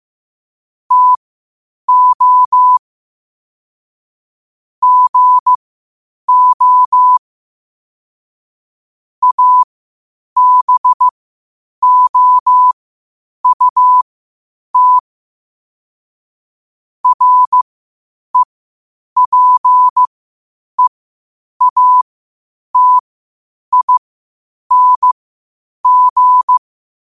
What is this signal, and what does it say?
Morse code, heard as a keyed tone, about 15 wpm, spaced as for 8 wpm, 1000 Hz. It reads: TO GO ABOUT REPEATING
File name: qotdfiles / 8wpm